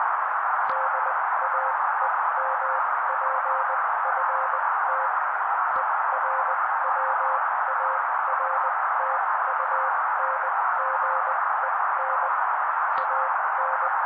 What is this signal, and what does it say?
Morse code, heard as a keyed tone, about 20 wpm, spaced as for 16 wpm, 530 Hz. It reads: DAEMPFT ERWARTUNGEN AN
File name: daily / 16wpm noise